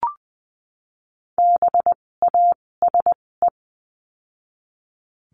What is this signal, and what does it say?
Morse code, heard as a keyed tone, about 20 wpm, spaced as for 16 wpm, 700 Hz. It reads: BASE
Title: 256 Lesson 42a - Words 12-20wpm